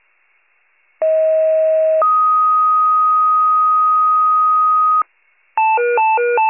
Two-tone page, heard as a one-second tone, then a three-second tone, about 640 then 1250 Hz. It is heard from 1.0 to 5.0 s.